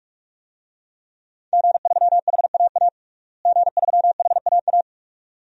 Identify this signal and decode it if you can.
Morse code, heard as a keyed tone, about 46 wpm, 700 Hz. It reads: G3HAA G3HAA
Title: DXCC-100 Call Signs - 5 Characters - Repeated 2x - 45wpm